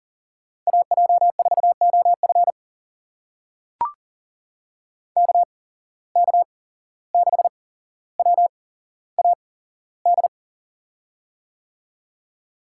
Morse code, heard as a keyed tone, about 40 wpm, spaced as for 16 wpm, 700 Hz. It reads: AJ4OF KK6WAD
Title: USA Call Signs - 8x Character Spacing - 40wpm